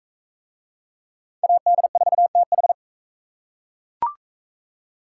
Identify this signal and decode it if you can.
Morse code, heard as a keyed tone, about 37 wpm, 700 Hz. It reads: AD4TH